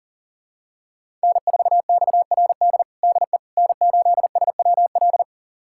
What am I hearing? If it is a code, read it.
Morse code, heard as a keyed tone, about 40 wpm, 700 Hz. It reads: N4XRD DE N8SWL
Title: USA Callsign DE Callsign 40wpm